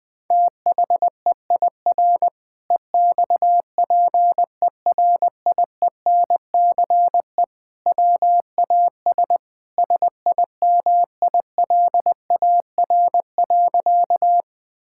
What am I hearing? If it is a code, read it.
Morse code, heard as a keyed tone, about 20 wpm, 700 Hz. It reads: THEIR EXPERIENCE WAS SIMILAR.
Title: Sentences from Top 1000 Words - Review Part 5 - 20wpm